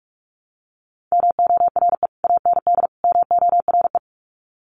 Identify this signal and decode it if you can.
Morse code, heard as a keyed tone, about 44 wpm, 700 Hz. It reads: MORE AND MORE
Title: Sets of 3 Words 45wpm